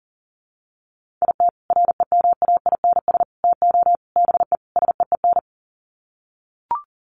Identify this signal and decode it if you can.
Morse code, heard as a keyed tone, about 40 wpm, 700 Hz. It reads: IT REMAINS TO BE SEEN
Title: Sets of 5 Words Encore 40wpm